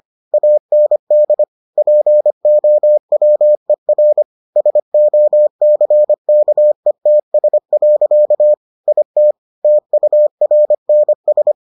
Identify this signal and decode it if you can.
Morse code, heard as a keyed tone, about 25 wpm, 600 Hz. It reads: AND POWER SOCKETS. IT TURNS